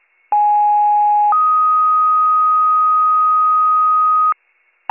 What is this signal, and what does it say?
Two-tone page, heard as a one-second tone, then a three-second tone, about 830 then 1250 Hz. It is heard from 0.3 to 4.3 s.